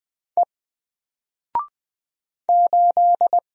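Morse code, heard as a keyed tone, about 20 wpm, 700 Hz. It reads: E 8
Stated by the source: Single letter and number Morse drill at 20 wpm, from N4LCD.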